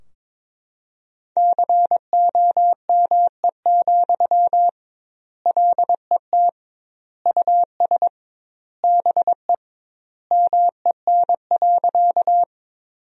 Morse code, heard as a keyed tone, about 22 wpm, 700 Hz. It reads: COME, LET US BE MEN.
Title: 2x Word-Spacing - Sentences from Top 300 Words - 22wpm